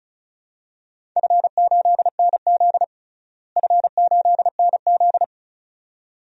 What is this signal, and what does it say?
Morse code, heard as a keyed tone, about 34 wpm, 700 Hz. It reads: F8NZ F8NZ
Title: DXCC-100 Call Signs - 4 Characters - Repeated 2x - 35wpm